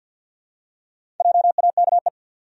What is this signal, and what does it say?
Morse code, heard as a keyed tone, about 46 wpm, 700 Hz. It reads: JAKE
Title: USA Common Names - Max 4 Letters - 50wpm